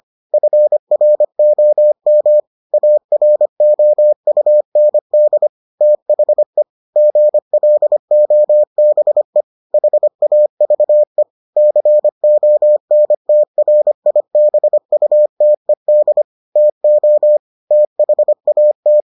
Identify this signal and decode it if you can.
Morse code, heard as a keyed tone, about 25 wpm, 600 Hz. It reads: FROM AROUND THE GLOBE HAVE CONTRIBUTED TO THAT